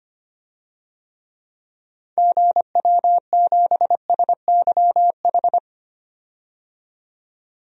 Morse code, heard as a keyed tone, about 25 wpm, 700 Hz. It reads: GW7SYH
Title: DXCC-100 Call Signs - 6 Characters - 25wpm